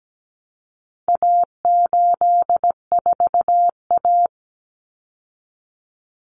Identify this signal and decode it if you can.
Morse code, heard as a keyed tone, about 17 wpm, 700 Hz. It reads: A84A
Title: Common Cut Numbers - 4-Digit 17wpm